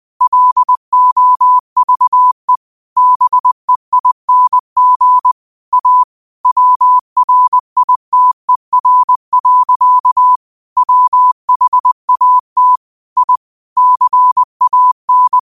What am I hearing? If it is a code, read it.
Morse code, heard as a keyed tone, about 20 wpm, 1000 Hz. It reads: LOVE BEING A WRITER. WHAT I CAN
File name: qotdfiles / 20wpm